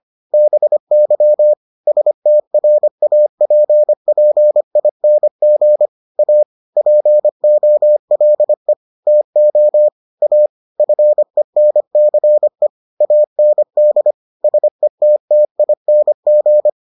Morse code, heard as a keyed tone, about 25 wpm, 600 Hz. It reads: BY STRAPPING A POLE TO A FENCE AND SETTING